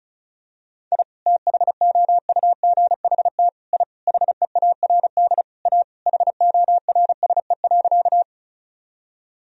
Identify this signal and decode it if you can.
Morse code, heard as a keyed tone, about 36 wpm, 700 Hz. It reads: I THOUGHT I HEARD A HORSE.
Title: Sentences from Top 1000 Words - Review Part 1 - 35wpm